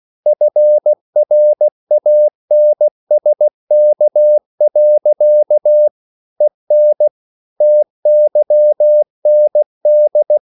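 Morse code, heard as a keyed tone, about 16 wpm, 600 Hz. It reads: FRANSK. EN TYND
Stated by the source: Slow (QRS) Morse code at 16 wpm, from CW med Gnister.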